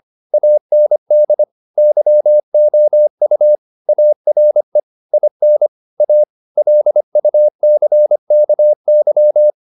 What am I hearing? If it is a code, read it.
Morse code, heard as a keyed tone, about 25 wpm, 600 Hz. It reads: AND YOU ARE IN A LUCKY
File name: the old man and the sea part 1 25wpm